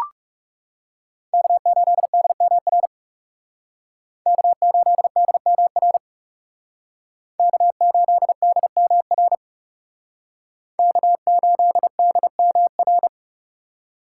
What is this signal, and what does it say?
Morse code, heard as a keyed tone, about 38 wpm, 700 Hz. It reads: K8DMR K8DMR K8DMR K8DMR